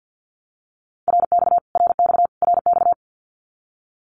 Morse code, heard as a keyed tone, about 50 wpm, 700 Hz. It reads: RX RX RX
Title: Abbreviations Repeated 3x - 50wpm